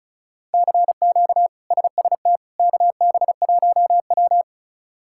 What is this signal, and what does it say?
Morse code, heard as a keyed tone, about 34 wpm, 700 Hz. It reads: CQ SST KB1W
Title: K1USN SST QSO Practice - 35wpm